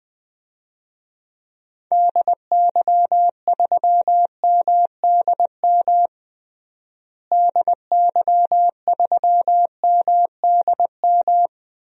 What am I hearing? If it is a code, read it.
Morse code, heard as a keyed tone, about 20 wpm, 700 Hz. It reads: DY3MDM DY3MDM
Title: DXCC-100 Call Signs - 6 Characters - Repeated 2x - 20wpm